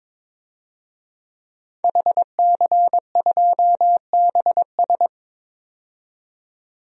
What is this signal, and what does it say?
Morse code, heard as a keyed tone, about 22 wpm, 700 Hz. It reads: HC2BS
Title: DXCC-100 Call Signs - 5 Characters - 22wpm